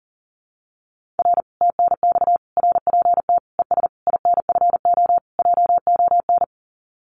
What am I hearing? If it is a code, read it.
Morse code, heard as a keyed tone, about 41 wpm, 700 Hz. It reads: R TNX RPT ES INFO JON
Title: Common QSO Phrases Encore 40wpm